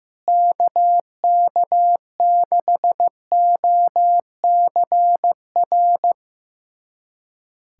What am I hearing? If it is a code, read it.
Morse code, heard as a keyed tone, about 15 wpm, 700 Hz. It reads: KK6OCR
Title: USA Callsign DE Callsign 15wpm